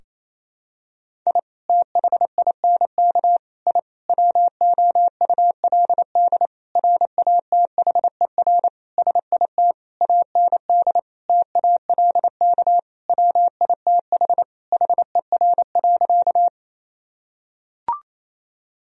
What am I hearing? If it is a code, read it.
Morse code, heard as a keyed tone, about 28 wpm, 700 Hz. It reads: I THINK I WOULD RATHER SIT AND TALK WITH HER.
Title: Sentences from Top 1000 Words - Review Part 4 - 28wpm